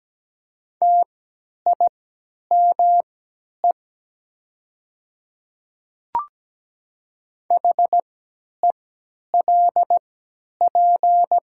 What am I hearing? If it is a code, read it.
Morse code, heard as a keyed tone, about 17 wpm, spaced as for 10 wpm, 700 Hz. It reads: TIME HELP
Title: 3x Character Spacing - Top 300 Words - 17wpm